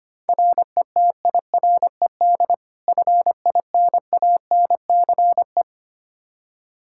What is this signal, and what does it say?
Morse code, heard as a keyed tone, about 25 wpm, 700 Hz. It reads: RETIRED FINANCE
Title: K1USN SST QSO Practice 25wpm